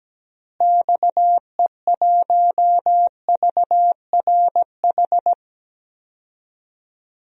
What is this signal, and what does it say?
Morse code, heard as a keyed tone, about 17 wpm, 700 Hz. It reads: XE1VRH
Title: DXCC-100 Call Signs - 6 Characters - Repeated 3x - 17wpm